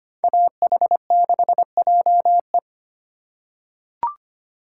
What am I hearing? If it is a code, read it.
Morse code, heard as a keyed tone, about 25 wpm, 700 Hz. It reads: AH6JE
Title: USA Call Signs - 5 Characters - 2x Character Spacing - 25wpm